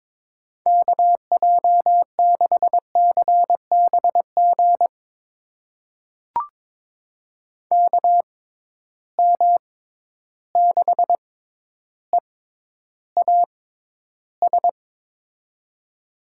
Morse code, heard as a keyed tone, about 21 wpm, spaced as for 11 wpm, 700 Hz. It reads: KJ6CBG KM6EAS
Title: USA Call Signs - 6 Characters - 6x Character Spacing - 22wpm